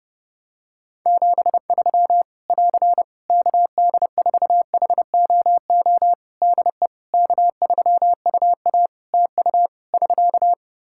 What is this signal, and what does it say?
Morse code, heard as a keyed tone, about 32 wpm, 700 Hz. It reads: 73 <AR> KD4HOO DE K3UA TU <SK>
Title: Practice QSO - Exchange 4 - Ending - 30wpm